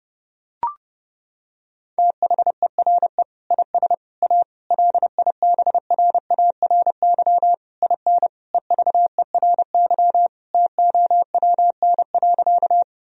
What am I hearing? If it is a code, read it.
Morse code, heard as a keyed tone, about 30 wpm, 700 Hz. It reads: THERE IS A LIBRARY IN EVERY TOWN.
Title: Sentences from Top 1000 Words - Review Part 4 - 30wpm